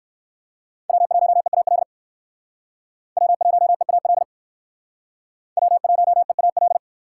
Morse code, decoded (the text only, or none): W1EAL W1EAL W1EAL